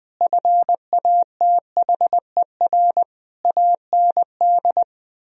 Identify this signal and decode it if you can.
Morse code, heard as a keyed tone, about 20 wpm, 700 Hz. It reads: FATHER AND
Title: Sentences from Top 1000 Words - Review Part 3 - 20wpm